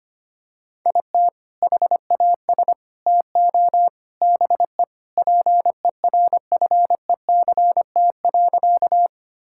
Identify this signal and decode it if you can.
Morse code, heard as a keyed tone, about 25 wpm, 700 Hz. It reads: IT HAS TO BE PERFECT.
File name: practice / 25wpm / Sentences with 3-Word Phrases - 25wpm